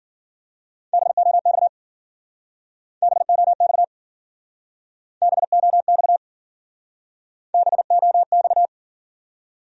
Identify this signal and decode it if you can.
Morse code, heard as a keyed tone, about 51 wpm, 700 Hz. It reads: BOX BOX BOX BOX